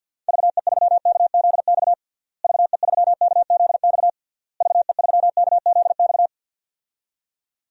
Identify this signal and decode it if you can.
Morse code, heard as a keyed tone, about 47 wpm, 700 Hz. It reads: VE3KZX VE3KZX VE3KZX
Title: DXCC-100 Call Signs - 6 Characters - Repeated 3x - 50wpm